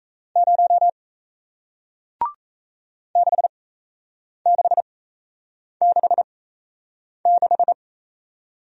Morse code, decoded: 0 6666

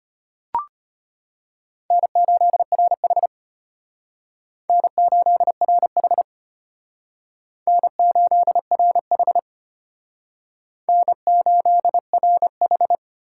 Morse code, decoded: N8RH N8RH N8RH N8RH

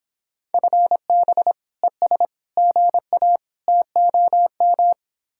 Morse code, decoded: FB ES GA TOM

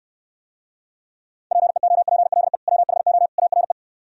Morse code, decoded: PEOPLE WHO ARE

